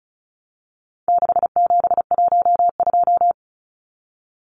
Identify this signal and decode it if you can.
Morse code, heard as a keyed tone, about 34 wpm, 700 Hz. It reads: 6712
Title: Numbers - 4-Digit 35wpm